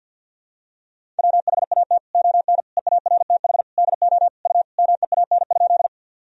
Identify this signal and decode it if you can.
Morse code, heard as a keyed tone, about 49 wpm, 700 Hz. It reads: WHAT ON EARTH DO U MEAN?
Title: Sentences from Top 200 Words and Abbreviations 50wpm